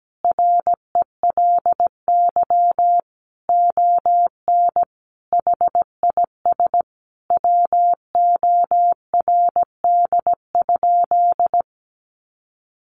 Morse code, heard as a keyed tone, about 17 wpm, 700 Hz. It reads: RELY ON HIS WORD?
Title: Sentences from Top 1200 Words 17wpm